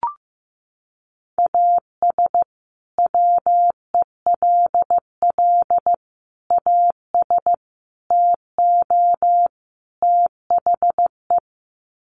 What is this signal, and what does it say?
Morse code, heard as a keyed tone, about 15 wpm, 700 Hz. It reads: AS WELL AS TO THE
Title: Sets of 5 Words Encore2 15wpm